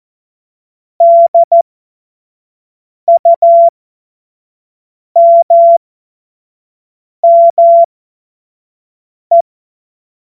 Morse code, heard as a keyed tone, about 14 wpm, spaced as for 5 wpm, 680 Hz. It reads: DUMME
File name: Morsekurs / Witze / Witz 16. Februar 2021, 25 F70